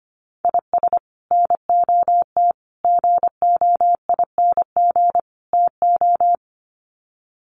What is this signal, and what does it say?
Morse code, heard as a keyed tone, about 25 wpm, 700 Hz. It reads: IS NOT GOING TO
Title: Sets of 4 Words 25wpm